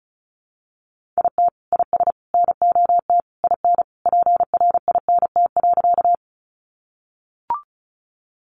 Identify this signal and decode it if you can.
Morse code, heard as a keyed tone, about 34 wpm, 700 Hz. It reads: IT IS NOT IN PRINT.
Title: Sentences from Top 1400 Words 35wpm